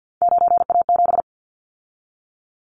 Morse code, heard as a keyed tone, about 52 wpm, 700 Hz. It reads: 9A7
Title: Callsign Prefixes for Top 50 DXCC - 50wpm